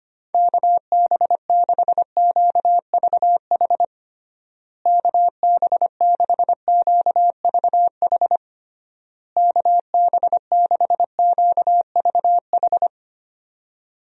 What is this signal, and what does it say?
Morse code, heard as a keyed tone, about 25 wpm, 700 Hz. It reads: KB6QVH KB6QVH KB6QVH
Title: USA Call Signs - 6 Characters - Repeated 3x - 25wpm